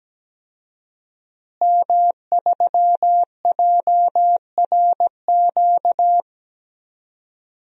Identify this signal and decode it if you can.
Morse code, heard as a keyed tone, about 17 wpm, 700 Hz. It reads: M3JRQ
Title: DXCC-100 Call Signs - 5 Characters - Repeated 2x - 17wpm